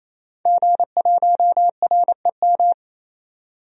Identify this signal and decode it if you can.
Morse code, heard as a keyed tone, about 28 wpm, 700 Hz. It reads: G1REM
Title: DXCC-100 Call Signs - 5 Characters - 28wpm